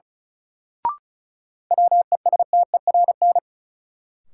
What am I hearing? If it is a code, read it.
Morse code, heard as a keyed tone, about 36 wpm, 700 Hz. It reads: WESTERN
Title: Sentences from Top 1000 Words - Review Part 7 - New Word Drill - 35wpm